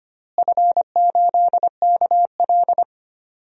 Morse code, heard as a keyed tone, about 25 wpm, 700 Hz. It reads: F8KL
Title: DXCC-100 Call Signs - 4 Characters - Repeated 3x - 25wpm